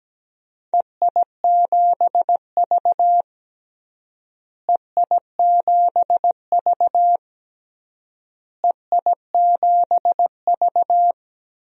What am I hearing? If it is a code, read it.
Morse code, heard as a keyed tone, about 17 wpm, 700 Hz. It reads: EI7V EI7V EI7V
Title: DXCC-100 Call Signs - 4 Characters - Repeated 3x - 17wpm